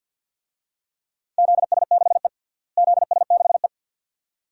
Morse code, heard as a keyed tone, about 51 wpm, 700 Hz. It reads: ZS6E ZS6E
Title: DXCC-100 Call Signs - 4 Characters - Repeated 2x - 50wpm